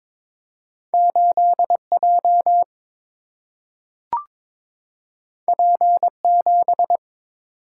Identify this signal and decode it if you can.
Morse code, heard as a keyed tone, about 22 wpm, 700 Hz. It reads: 8 J P 7
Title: Single-Letter Single-Number 22wpm